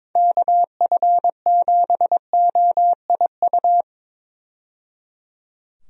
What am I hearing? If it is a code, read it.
Morse code, heard as a keyed tone, about 22 wpm, 700 Hz. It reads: KF7OIU
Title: USA Call Signs - 6 Characters - 22wpm